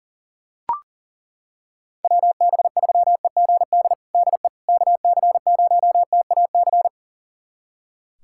Morse code, heard as a keyed tone, about 37 wpm, 700 Hz. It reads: WB3EGD DE KC0TAC
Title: USA Callsign DE Callsign - 6 Characters - 40wpm